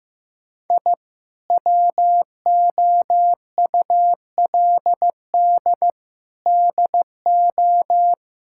Morse code, heard as a keyed tone, about 15 wpm, 700 Hz. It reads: I WOULD DO
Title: Sentences from Top 1000 Words - Review Part 6 - 15wpm